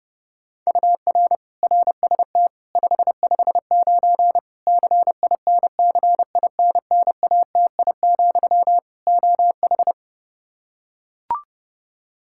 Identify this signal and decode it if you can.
Morse code, heard as a keyed tone, about 30 wpm, 700 Hz. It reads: UR RST 559 CINCINNATI, OH
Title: Practice QSO - Exchange 1 - Beginning - 30wpm